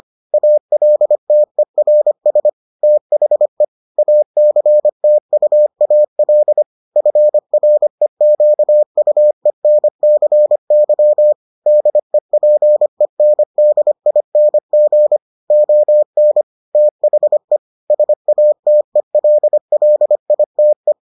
Morse code, heard as a keyed tone, about 25 wpm, 600 Hz. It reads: ALTERS THE ACTUAL FREQUENCY DEPENDING ON THE SATELLITE